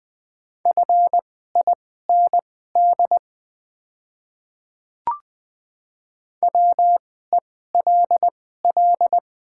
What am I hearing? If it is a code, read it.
Morse code, heard as a keyed tone, about 20 wpm, spaced as for 14 wpm, 700 Hz. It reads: FIND WELL
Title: 2x Character Spacing - Top 300 Words - 20wpm